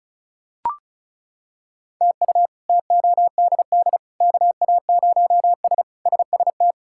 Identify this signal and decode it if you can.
Morse code, heard as a keyed tone, about 34 wpm, 700 Hz. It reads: TU TODD KA0S SST
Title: K1USN SST QSO Practice - 35wpm